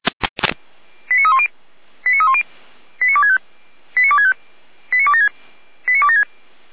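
Five-tone selective call (ZVEI), heard as quick runs of five tones, tones of about 70 ms. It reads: 89310 89310 89265 89265 89276 89276
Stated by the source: ZVEI five-tone selective call 89250 received by SDR, archived by feuerwehr-orlamuende fire brigade server.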